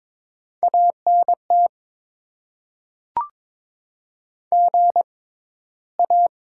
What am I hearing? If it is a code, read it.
Morse code, heard as a keyed tone, about 22 wpm, spaced as for 13 wpm, 700 Hz. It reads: ANT GA
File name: Top 20 QSO Elements - 6x Character Spacing - 22wpm